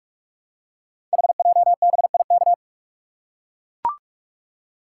Morse code, heard as a keyed tone, about 44 wpm, 700 Hz. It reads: HJBIK
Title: Code Groups - 5 Letters - 45wpm